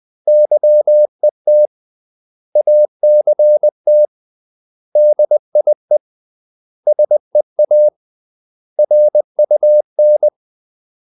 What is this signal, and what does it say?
Morse code, heard as a keyed tone, about 20 wpm, 600 Hz. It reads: YET ACT DIE SEA RUN
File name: CW / Practice / 500MostCommonEnglishWords20